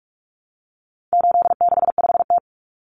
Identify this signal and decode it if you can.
Morse code, heard as a keyed tone, about 44 wpm, 700 Hz. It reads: 865T
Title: Common Cut Numbers - 4-Digit 45wpm